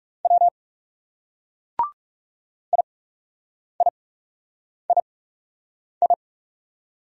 Morse code, heard as a keyed tone, about 40 wpm, spaced as for 8 wpm, 700 Hz. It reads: W IIII